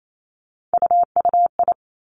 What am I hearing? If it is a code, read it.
Morse code, heard as a keyed tone, about 28 wpm, 700 Hz. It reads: UUI
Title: Three Challenging Letter-Number Combinations 28wpm